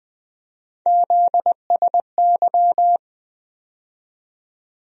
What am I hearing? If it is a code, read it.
Morse code, heard as a keyed tone, about 20 wpm, 700 Hz. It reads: ZSY
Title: USA Call Sign Postfixes - 3 Letters 20wpm